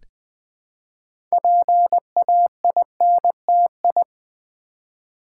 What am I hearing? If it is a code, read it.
Morse code, heard as a keyed tone, about 20 wpm, 700 Hz. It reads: PAINTI
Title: Build A Word - 2 Letters At A Time - 8-Letter Words - 20wpm